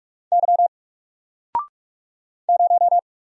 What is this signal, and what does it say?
Morse code, heard as a keyed tone, about 44 wpm, 700 Hz. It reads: Y 0
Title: Single Letter-Number Alt 45wpm